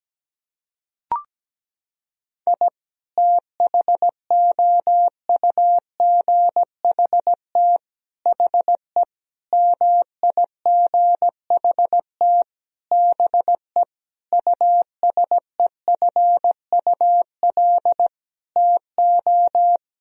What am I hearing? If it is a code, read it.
Morse code, heard as a keyed tone, about 17 wpm, 700 Hz. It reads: I THOUGHT HE MIGHT BE USEFUL TO
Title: Sentences from Top 1000 Words - Review Part 4 - 17wpm